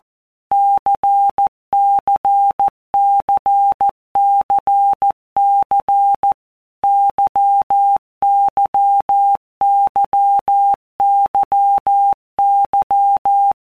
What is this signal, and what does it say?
Morse code, heard as a keyed tone, about 14 wpm, 800 Hz. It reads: CCCCC YYYYY
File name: Basic Letter Groups By WPM's / Groups@13 / LFCYQ@13